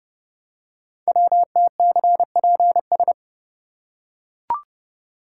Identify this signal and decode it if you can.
Morse code, heard as a keyed tone, about 30 wpm, 700 Hz. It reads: WTCPS
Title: Code Groups - 5 Letters - 30wpm